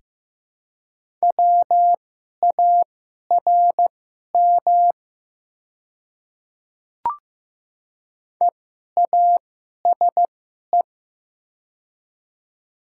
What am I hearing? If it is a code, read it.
Morse code, heard as a keyed tone, about 15 wpm, spaced as for 11 wpm, 700 Hz. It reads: WARM EASE
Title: 2x Character Spacing - Top 500 Words - 15wpm